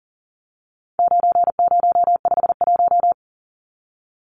Morse code, heard as a keyed tone, about 40 wpm, 700 Hz. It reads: 9051
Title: Numbers - 4-Digit 40wpm